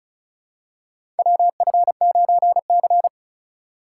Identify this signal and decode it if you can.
Morse code, heard as a keyed tone, about 34 wpm, 700 Hz. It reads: WF9C
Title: USA Call Signs - 4 Characters - 4x Character Spacing - 35wpm